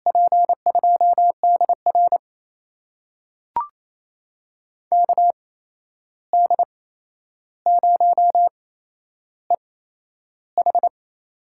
Morse code, heard as a keyed tone, about 28 wpm, spaced as for 11 wpm, 700 Hz. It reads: P2DR KD0EH
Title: USA Call Signs - 5 Characters - 8x Character Spacing - 28wpm